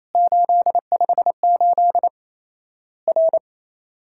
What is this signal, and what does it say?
Morse code, heard as a keyed tone, about 28 wpm, 700 Hz. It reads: 858 R
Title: Worked All Europe DX Contest - QTC Exchange - 28wpm